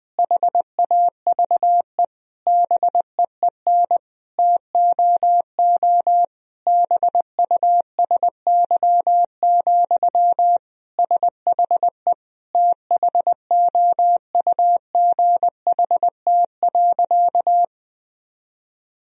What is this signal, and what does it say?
Morse code, heard as a keyed tone, about 21 wpm, 700 Hz. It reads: HAVE BEEN TOO BUSY, SHE THOUGHT.
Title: Sentences from Top 1000 Words - Review Part 3 - 20wpm